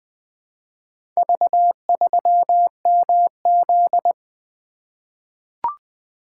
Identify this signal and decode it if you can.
Morse code, heard as a keyed tone, about 20 wpm, 700 Hz. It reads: V3MZ
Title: Four Letter-Number Combinations - 20wpm